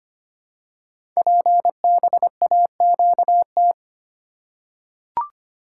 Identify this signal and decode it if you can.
Morse code, heard as a keyed tone, about 25 wpm, 700 Hz. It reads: PBAQT
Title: Code Groups - 5 Letters - 25wpm